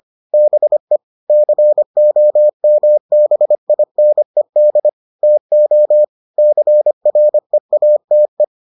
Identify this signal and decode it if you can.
Morse code, heard as a keyed tone, about 25 wpm, 600 Hz. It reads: BE COMBINED TO CREATE